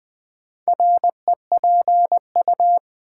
Morse code, heard as a keyed tone, about 20 wpm, 700 Hz. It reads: REPU
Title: Build A Word - 4 Letters At A Time - 10-Letter Words - 20wpm